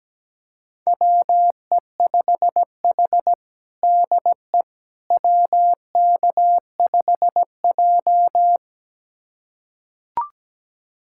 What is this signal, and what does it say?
Morse code, heard as a keyed tone, about 17 wpm, 700 Hz. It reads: WE5H DE WK5J
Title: USA Callsign DE Callsign - 4 Characters - 17wpm